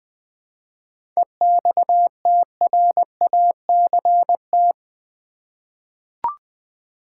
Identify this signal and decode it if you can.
Morse code, heard as a keyed tone, about 20 wpm, 700 Hz. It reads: EXTRACT
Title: 7 Letter Words - First 3 Letters Given - 20wpm